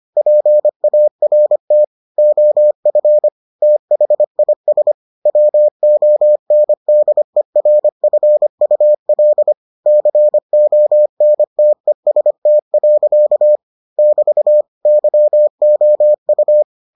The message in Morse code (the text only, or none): PART OF THIS WONDERFUL CONTEST. <BT> YOU